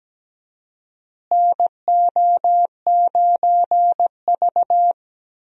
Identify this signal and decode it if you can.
Morse code, heard as a keyed tone, about 17 wpm, 700 Hz. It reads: NO9V
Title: USA Call Signs - 4 Characters - Repeated 2x - 17wpm